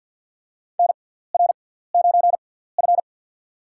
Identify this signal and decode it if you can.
Morse code, heard as a keyed tone, about 48 wpm, spaced as for 17 wpm, 700 Hz. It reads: NR9F
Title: USA Call Signs - 4 Characters - 6x Character Spacing - 50wpm